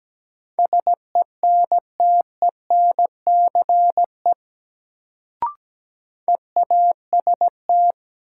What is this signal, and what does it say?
Morse code, heard as a keyed tone, about 17 wpm, 700 Hz. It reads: SENTENCE EAST
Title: Sentences from Top 1200 Words - New Word Drill Alt - 17wpm